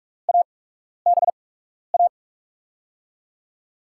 Morse code, heard as a keyed tone, about 47 wpm, spaced as for 12 wpm, 700 Hz. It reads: ABA
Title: USA Call Signs - 5 Characters - 8x Character Spacing - 45wpm